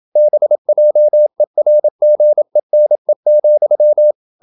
Morse code, heard as a keyed tone, about 27 wpm, 600 Hz. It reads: BJERGENE,
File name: Ferie QRQ 27wpm